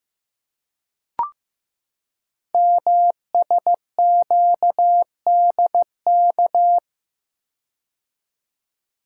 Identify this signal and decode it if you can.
Morse code, heard as a keyed tone, about 15 wpm, 700 Hz. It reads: MSQDK
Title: Code Groups - 5 Letters - 15wpm